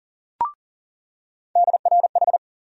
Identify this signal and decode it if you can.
Morse code, heard as a keyed tone, about 40 wpm, 700 Hz. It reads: DRH
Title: USA Call Sign Postfixes - 3 Letters 40wpm